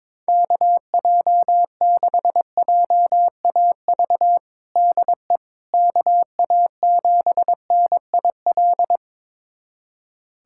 Morse code, heard as a keyed tone, about 22 wpm, 700 Hz. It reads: KJ6JAV DE KA7NIL
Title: USA Callsign DE Callsign 22wpm